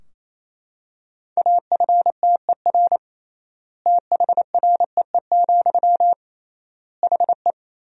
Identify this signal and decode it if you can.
Morse code, heard as a keyed tone, about 28 wpm, 700 Hz. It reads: AFTER THREE, HE
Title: 3x Word-Spacing - Sentences from Top 200 Words - 28wpm